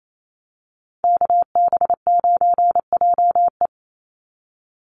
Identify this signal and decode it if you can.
Morse code, heard as a keyed tone, about 28 wpm, 700 Hz. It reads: KB9JE
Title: USA Call Signs 28wpm